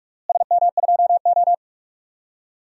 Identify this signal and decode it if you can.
Morse code, heard as a keyed tone, about 46 wpm, 700 Hz. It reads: SM2O
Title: DXCC-100 Call Signs - 4 Characters - Repeated 3x - 45wpm